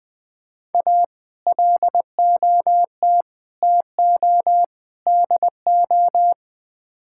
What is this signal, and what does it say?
Morse code, heard as a keyed tone, about 20 wpm, 700 Hz. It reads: A LOT TO DO
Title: Build A Phrase - 5-Word Phrases - 20wpm